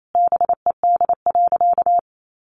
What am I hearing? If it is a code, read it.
Morse code, heard as a keyed tone, about 28 wpm, 700 Hz. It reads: BED.
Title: Sentences from Top 1100 Words 28wpm